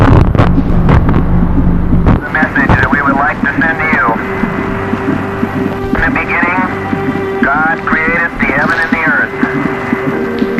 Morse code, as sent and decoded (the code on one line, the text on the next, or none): none